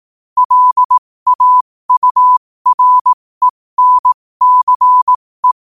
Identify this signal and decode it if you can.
Morse code, heard as a keyed tone, about 18 wpm, spaced as for 15 wpm, 1000 Hz. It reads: LAURENCE
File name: qotdfiles / 15wpm